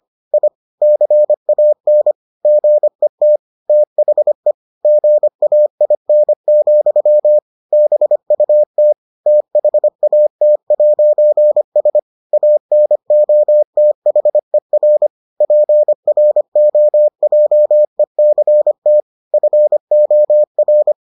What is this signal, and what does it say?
Morse code, heard as a keyed tone, about 25 wpm, 600 Hz. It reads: I CAN GET THE GAIN, BUT THAT'S ANOTHER PROJECT FOR